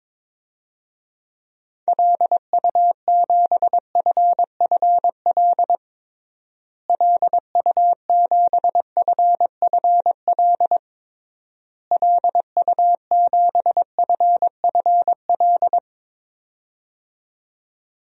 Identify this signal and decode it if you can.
Morse code, heard as a keyed tone, about 22 wpm, 700 Hz. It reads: LU7FFL LU7FFL LU7FFL
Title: DXCC-100 Call Signs - 6 Characters - Repeated 3x - 22wpm